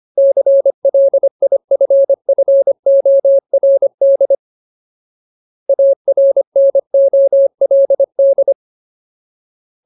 Morse code, heard as a keyed tone, about 25 wpm, 550 Hz. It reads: CLIFFORD ARNOLD
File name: Ultimate Word List / 25WPM Ultimate